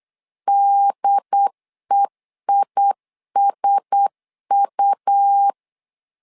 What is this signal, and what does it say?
Morse code, heard as a keyed tone, about 8 wpm, 800 Hz. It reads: DEISU
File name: LEZIONI / 4 Q1S 8wpm 2710